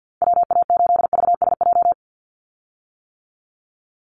Morse code, heard as a keyed tone, about 50 wpm, 700 Hz. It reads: WA8VSJ
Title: Call Signs - 50wpm